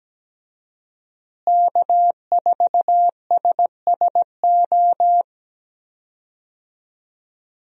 Morse code, heard as a keyed tone, about 17 wpm, 700 Hz. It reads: K4SSO
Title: USA Call Signs - 5 Characters - Repeated 2x - 17wpm